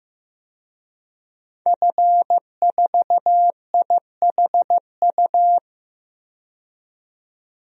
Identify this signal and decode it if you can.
Morse code, heard as a keyed tone, about 15 wpm, 700 Hz. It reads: F4IHU
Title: DXCC-100 Call Signs - 5 Characters - 15wpm